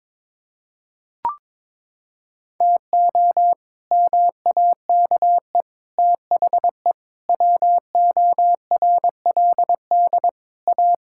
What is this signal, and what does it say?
Morse code, heard as a keyed tone, about 21 wpm, 700 Hz. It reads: TO MAKE THE WORLD A
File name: practice / 22wpm / Sets of 7 Words - 22wpm